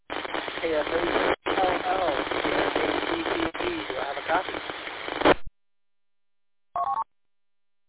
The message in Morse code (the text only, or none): none